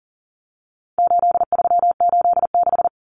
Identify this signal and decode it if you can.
Morse code, heard as a keyed tone, about 40 wpm, 700 Hz. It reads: 8386
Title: Numbers - 4-Digit 40wpm